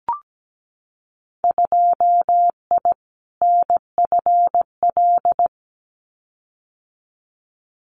Morse code, heard as a keyed tone, about 17 wpm, 700 Hz. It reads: 2I NFL
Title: Winter Field Day Exchanges 17wpm